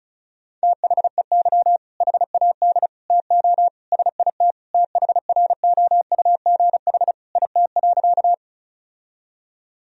Morse code, heard as a keyed tone, about 36 wpm, 700 Hz. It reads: THEY HAD TO SIT THROUGH IT.
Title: Sentences from Top 1000 Words - Review Part 1 - 35wpm